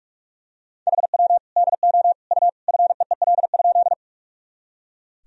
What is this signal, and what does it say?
Morse code, heard as a keyed tone, about 41 wpm, 700 Hz. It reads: HW DO U FEEL?
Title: Sentences from Top 400 Words and Abbreviations 45wpm